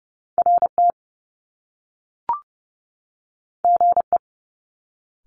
Morse code, heard as a keyed tone, about 30 wpm, 700 Hz. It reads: RT GE
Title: Top Two Letter Combinations in Words 30wpm